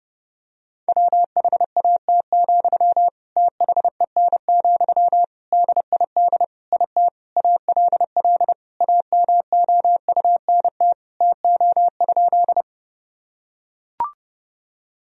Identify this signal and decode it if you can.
Morse code, heard as a keyed tone, about 30 wpm, 700 Hz. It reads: WHAT, THEN, DID IT ALL AMOUNT TO?
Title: Sentences from Top 1000 Words - Review Part 4 - 30wpm